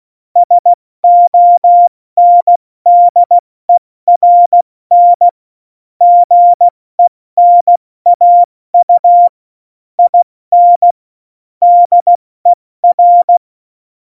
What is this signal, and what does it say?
Morse code, heard as a keyed tone, about 16 wpm, spaced as for 14 wpm, 700 Hz. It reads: SONDERN GENAU IN DER
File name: Morsekurs / Witze / Witz 23. April 2019, 70 F80